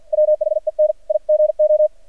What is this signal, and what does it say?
Morse code, heard as a keyed tone, about 54 wpm, 610 Hz. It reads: WHEN I GO
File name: archived / CRSnet